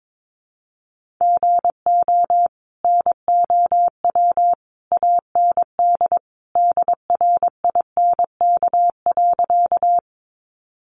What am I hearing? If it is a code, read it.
Morse code, heard as a keyed tone, about 22 wpm, 700 Hz. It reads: GO NOW AND DRINK.
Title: Sentences from Top 1500 Words 22wpm